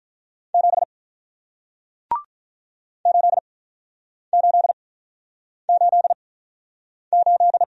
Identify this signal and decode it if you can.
Morse code, heard as a keyed tone, about 48 wpm, spaced as for 9 wpm, 700 Hz. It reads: 7 8888